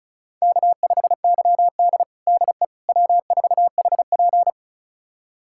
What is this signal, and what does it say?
Morse code, heard as a keyed tone, about 36 wpm, 700 Hz. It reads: K5YD DE W4HP
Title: USA Callsign DE Callsign - 4 Characters - 35wpm